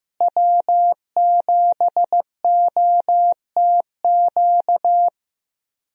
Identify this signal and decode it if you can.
Morse code, heard as a keyed tone, about 15 wpm, 700 Hz. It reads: W7OTQ
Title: USA Call Signs - 5 Characters - Repeated 2x - 15wpm